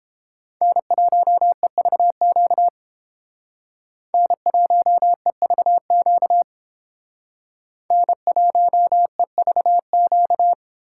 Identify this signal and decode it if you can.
Morse code, heard as a keyed tone, about 31 wpm, 700 Hz. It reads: N1EVQ N1EVQ N1EVQ